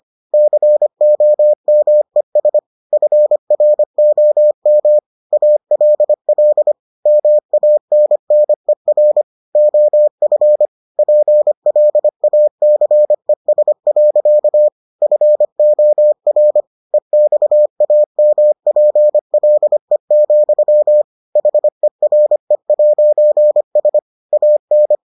Morse code, heard as a keyed tone, about 25 wpm, 600 Hz. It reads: COMES FROM ALL MANNER OF PLACES. FOR EXAMPLE, HERE'S AN